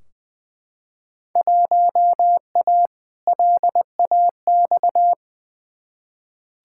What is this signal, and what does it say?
Morse code, heard as a keyed tone, about 20 wpm, 700 Hz. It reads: 1A LAX
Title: Field Day Exchanges - 20wpm